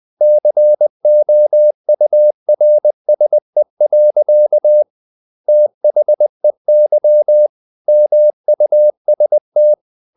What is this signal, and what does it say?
Morse code, heard as a keyed tone, about 20 wpm, 600 Hz. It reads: COURSE. THEY MUST
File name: the old man and the sea part 2 20wpm